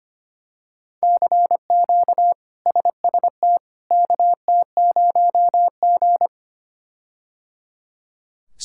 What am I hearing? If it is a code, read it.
Morse code, heard as a keyed tone, about 25 wpm, 700 Hz. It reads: CQ SST KT0G
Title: K1USN SST QSO Practice - 25wpm